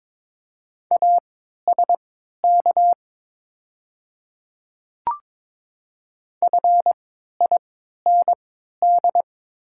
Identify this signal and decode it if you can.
Morse code, heard as a keyed tone, about 22 wpm, spaced as for 12 wpm, 700 Hz. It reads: ASK FIND
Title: 3x Character Spacing - Top 200 Words - 22wpm